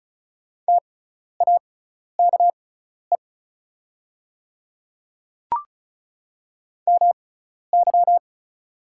Morse code, heard as a keyed tone, about 36 wpm, spaced as for 12 wpm, 700 Hz. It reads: TAKE MY
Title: 6x Character Spacing - Top 200 Words - 35wpm